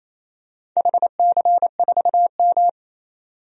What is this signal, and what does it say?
Morse code, heard as a keyed tone, about 28 wpm, 700 Hz. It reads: HC4M
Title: DXCC-100 Call Signs - 4 Characters - 28wpm